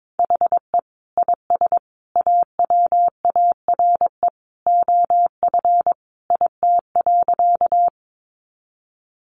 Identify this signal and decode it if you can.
Morse code, heard as a keyed tone, about 22 wpm, 700 Hz. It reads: HE IS AWARE OF IT.
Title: Sentences from Top 1200 Words 22wpm